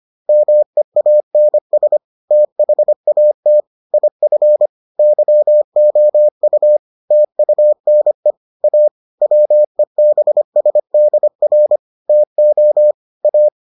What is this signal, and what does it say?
Morse code, heard as a keyed tone, about 25 wpm, 600 Hz. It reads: MEANS THAT IF YOU TUNE A WEBSDR TO A